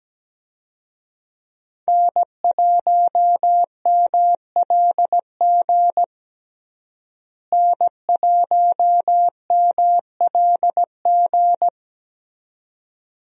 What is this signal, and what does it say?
Morse code, heard as a keyed tone, about 17 wpm, 700 Hz. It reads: N1MLG N1MLG
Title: USA Call Signs - 5 Characters - Repeated 2x - 17wpm